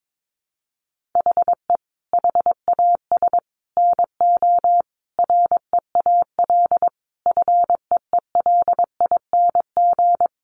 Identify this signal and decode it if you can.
Morse code, heard as a keyed tone, about 22 wpm, 700 Hz. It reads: HE HAS NO REAL FEELING
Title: Sentences from Top 1200 Words 22wpm